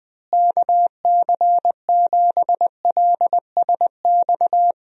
Morse code, heard as a keyed tone, about 21 wpm, 700 Hz. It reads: KC7LSX
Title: USA Call Signs - 6 Characters - Repeated 3x - 20wpm